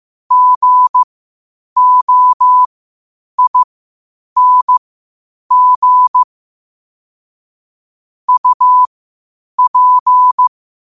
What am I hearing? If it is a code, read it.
Morse code, heard as a keyed tone, about 15 wpm, spaced as for 8 wpm, 1000 Hz. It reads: GOING UP
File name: qotdfiles / 8wpm